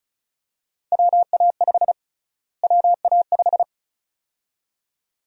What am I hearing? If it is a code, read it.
Morse code, heard as a keyed tone, about 36 wpm, 700 Hz. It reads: WA5 WA5
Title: USA Call Sign Prefixes - Repeated 2x - 35wpm